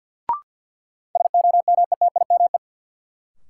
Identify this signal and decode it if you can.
Morse code, heard as a keyed tone, about 50 wpm, 700 Hz. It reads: SOMETIME
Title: Sentences from Top 1000 Words - Review Part 6 - New Word Drill - 50wpm